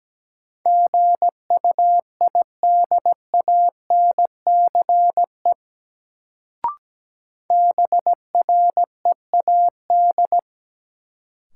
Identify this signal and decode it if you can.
Morse code, heard as a keyed tone, about 17 wpm, 700 Hz. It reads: GUIDANCE BREAD
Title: Sentences from Top 1600 Words - New Word Drill - 17wpm